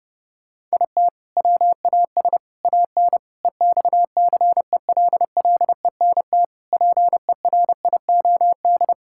Morse code, heard as a keyed tone, about 30 wpm, 700 Hz. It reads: IT WAS AN EXCELLENT PERIOD
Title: Sentences from Top 1000 Words - Review Part 5 - 30wpm